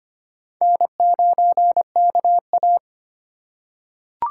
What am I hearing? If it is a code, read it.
Morse code, heard as a keyed tone, about 25 wpm, 700 Hz. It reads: N9KA
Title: USA Call Signs - 2x Character Spacing - 25wpm